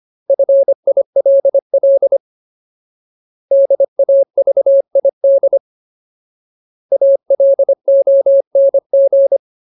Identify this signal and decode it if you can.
Morse code, heard as a keyed tone, about 25 wpm, 550 Hz. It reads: FILL DAVID ALONG
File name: Ultimate Word List / 25WPM Ultimate